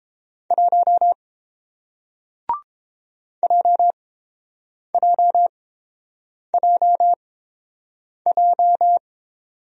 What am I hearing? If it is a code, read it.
Morse code, heard as a keyed tone, about 30 wpm, spaced as for 7 wpm, 700 Hz. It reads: 1 JJJJ